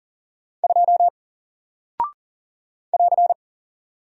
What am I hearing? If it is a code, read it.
Morse code, heard as a keyed tone, about 40 wpm, 700 Hz. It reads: 2 <AR>
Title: Single Character (Letter:Number:Punctuation:Prosign) Alt 40wpm